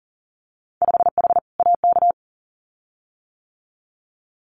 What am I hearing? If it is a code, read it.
Morse code, heard as a keyed tone, about 42 wpm, 700 Hz. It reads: 5H AK
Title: Winter Field Day Exchanges Encore 40wpm